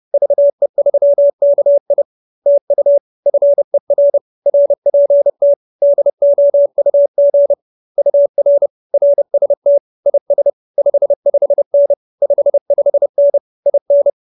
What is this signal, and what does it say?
Morse code, heard as a keyed tone, about 30 wpm, 570 Hz. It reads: VE3KI TU FER RPT DOUG UR RST IS 55N 55N IN